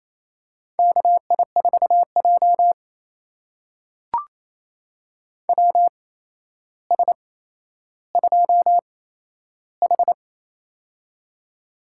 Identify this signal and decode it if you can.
Morse code, heard as a keyed tone, about 28 wpm, spaced as for 12 wpm, 700 Hz. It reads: KI4J WS2H
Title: USA Call Signs - 4 Characters - 8x Character Spacing - 28wpm